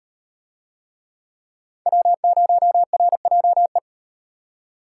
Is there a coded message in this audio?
Yes: Morse code from 1.9 to 3.8 s, heard as a keyed tone, about 38 wpm, 700 Hz. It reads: W0RJE